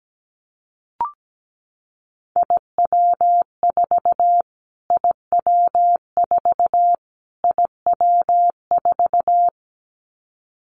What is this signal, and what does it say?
Morse code, heard as a keyed tone, about 17 wpm, 700 Hz. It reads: IW4 IW4 IW4
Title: Callsign Prefixes for Top 25 DXCC - Repeated 3x - 17wpm